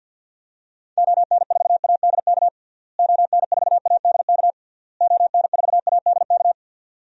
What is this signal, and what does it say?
Morse code, heard as a keyed tone, about 50 wpm, 700 Hz. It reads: ON4ADK ON4ADK ON4ADK
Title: DXCC-100 Call Signs - 6 Characters - Repeated 3x - 50wpm